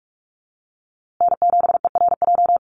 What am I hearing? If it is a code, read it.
Morse code, heard as a keyed tone, about 44 wpm, 700 Hz. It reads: N7ERJ
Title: USA Call Signs Repeated 2x 45wpm